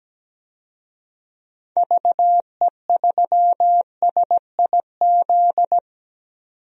Morse code, heard as a keyed tone, about 17 wpm, 700 Hz. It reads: VE3SIZ
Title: DXCC-100 Call Signs - 6 Characters - 17wpm